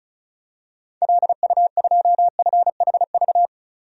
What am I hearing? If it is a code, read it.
Morse code, heard as a keyed tone, about 36 wpm, 700 Hz. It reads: LU2FHV
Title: DXCC-100 Call Signs - 6 Characters - 35wpm